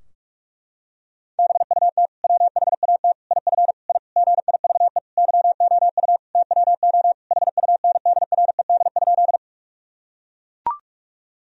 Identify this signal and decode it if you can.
Morse code, heard as a keyed tone, about 41 wpm, 700 Hz. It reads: BUT WHAT IF I GIVE YOU TWO HUNDRED?
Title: Sentences from Top 1000 Words - Review Part 2 - 45wpm